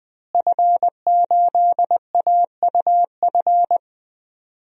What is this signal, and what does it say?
Morse code, heard as a keyed tone, about 20 wpm, 700 Hz. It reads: F8AUF
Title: DXCC-100 Call Signs - 5 Characters - Repeated 2x - 20wpm